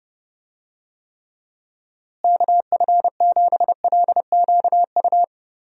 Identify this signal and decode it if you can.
Morse code, heard as a keyed tone, about 29 wpm, 700 Hz. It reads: KF7LQU
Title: USA Call Signs - 6 Characters - Repeated 2x - 30wpm